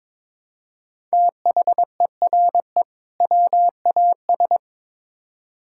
Morse code, heard as a keyed tone, about 23 wpm, 700 Hz. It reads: THERE WAS
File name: Build A Phrase - 2-Word Phrases - 22wpm